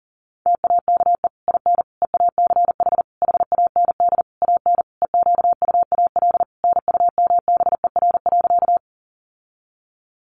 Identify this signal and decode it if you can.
Morse code, heard as a keyed tone, about 40 wpm, 700 Hz. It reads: TAKE IN EACH HAND AN EQUAL NUMBER.
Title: Sentences from Top 1400 Words 40wpm